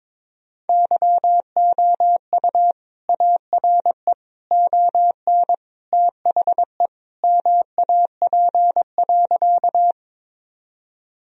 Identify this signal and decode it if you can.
Morse code, heard as a keyed tone, about 22 wpm, 700 Hz. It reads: YOU ARE ON THE MAP.
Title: Sentences from Top 1000 Words - Review Part 2 - 22wpm